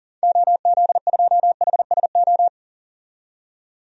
Morse code, heard as a keyed tone, about 40 wpm, 700 Hz. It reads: OZ2HSO
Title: DXCC-100 Call Signs - 6 Characters - 40wpm